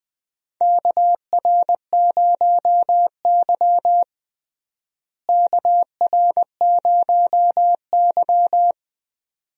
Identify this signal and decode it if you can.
Morse code, heard as a keyed tone, about 20 wpm, 700 Hz. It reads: KR0Y KR0Y
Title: USA Call Signs - 4 Characters - Repeated 2x - 20wpm